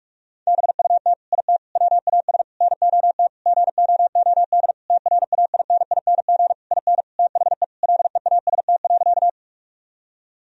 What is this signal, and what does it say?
Morse code, heard as a keyed tone, about 46 wpm, 700 Hz. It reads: BUT IT WAS NOT GOOD TRAINING IN THE LEAST.
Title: Sentences from Top 1000 Words - Review Part 4 - 45wpm